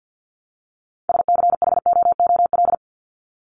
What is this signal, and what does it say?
Morse code, heard as a keyed tone, about 48 wpm, 700 Hz. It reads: SCHOOL
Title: Top 300 Words 50wpm